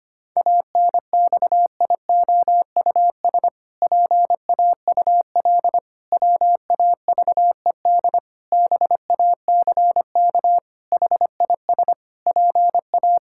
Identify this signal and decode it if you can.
Morse code, heard as a keyed tone, about 25 wpm, 700 Hz. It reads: ANXIOUS PAUL WAVED BACK HIS PA
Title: Pangram Challenge v2 - 25wpm